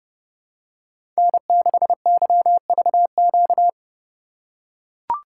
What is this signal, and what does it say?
Morse code, heard as a keyed tone, about 29 wpm, 700 Hz. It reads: N6YVQ
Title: USA Call Signs - 5 Characters - 3x Character Spacing - 30wpm